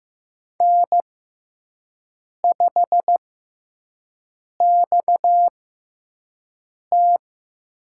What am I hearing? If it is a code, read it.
Morse code, heard as a keyed tone, about 15 wpm, spaced as for 5 wpm, 700 Hz. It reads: N5XT
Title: USA Call Signs - 4 Characters - 6x Character Spacing - 15wpm